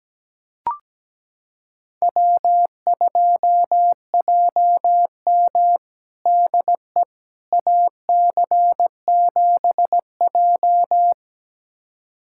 Morse code, heard as a keyed tone, about 17 wpm, 700 Hz. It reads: W2JM DE AC7J
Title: USA Callsign DE Callsign 17wpm